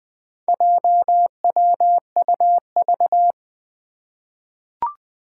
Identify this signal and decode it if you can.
Morse code, heard as a keyed tone, about 20 wpm, 700 Hz. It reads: JWUV
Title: Four Challenging Letter-Number Combinations 20wpm